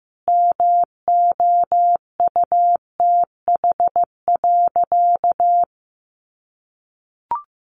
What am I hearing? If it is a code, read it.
Morse code, heard as a keyed tone, about 15 wpm, 700 Hz. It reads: MOUTH.
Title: Sentences from Top 1500 Words 15wpm